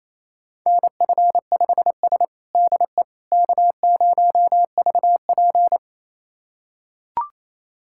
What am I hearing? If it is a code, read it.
Morse code, heard as a keyed tone, about 28 wpm, 700 Hz. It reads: NF5S DE K0VP
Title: USA Callsign DE Callsign - 4 Characters - 28wpm